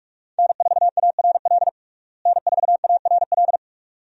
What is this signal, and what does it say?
Morse code, heard as a keyed tone, about 46 wpm, 700 Hz. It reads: N4ARL N4ARL
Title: USA Call Signs - 5 Characters - Repeated 2x - 45wpm